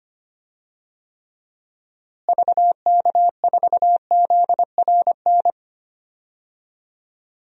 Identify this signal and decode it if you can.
Morse code, heard as a keyed tone, about 25 wpm, 700 Hz. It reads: VK4ZRN
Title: DXCC-100 Call Signs - 6 Characters - Repeated 2x - 25wpm